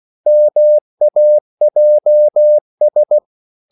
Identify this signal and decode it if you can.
Morse code, heard as a keyed tone, about 16 wpm, 600 Hz. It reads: MAJS
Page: Korte danske ord 16wpm